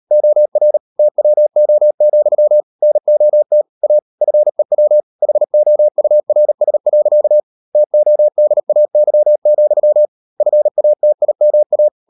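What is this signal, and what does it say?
Morse code, heard as a keyed tone, about 38 wpm, 600 Hz. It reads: OR TWO, NOT A FEW HOURS. TODAY, FATIMA
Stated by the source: Morse code practice at 38 WPM, sending Ken Follett's novel Never in English, from CW med Gnister.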